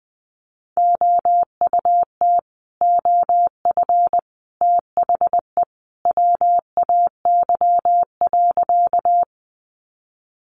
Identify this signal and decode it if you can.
Morse code, heard as a keyed tone, about 20 wpm, 700 Hz. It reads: OUT OF THE WAY.
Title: Sentences from Top 100 Words 20wpm